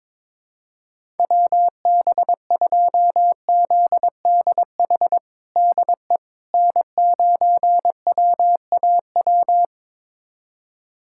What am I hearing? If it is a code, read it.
Morse code, heard as a keyed tone, about 22 wpm, 700 Hz. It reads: WB2ZDH DE N9WAW
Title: USA Callsign DE Callsign 22wpm